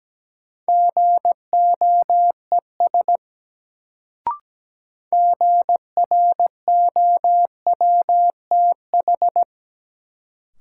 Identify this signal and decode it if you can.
Morse code, heard as a keyed tone, about 17 wpm, 700 Hz. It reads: GOES GROWTH
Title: Sentences from Top 1000 Words - Review Part 5 - New Word Drill - 17wpm